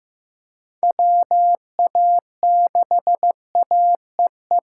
Morse code, heard as a keyed tone, about 15 wpm, 700 Hz. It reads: WA6AEE
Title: K1USN SST QSO Practice - 15wpm